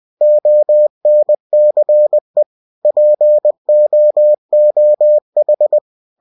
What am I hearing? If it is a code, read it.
Morse code, heard as a keyed tone, about 20 wpm, 600 Hz. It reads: ONCE POOH
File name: cw-books / winnie-the-pooh / winnie-20wpm-page-0018